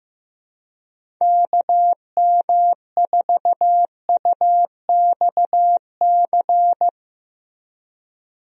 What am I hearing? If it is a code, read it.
Morse code, heard as a keyed tone, about 15 wpm, 700 Hz. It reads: KM4UXC
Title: USA Call Signs - 6 Characters - Repeated 2x - 15wpm